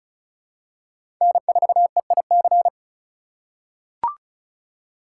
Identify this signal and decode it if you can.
Morse code, heard as a keyed tone, about 36 wpm, 700 Hz. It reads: N4EIC
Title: USA Call Signs - 5 Characters - 2x Character Spacing - 35wpm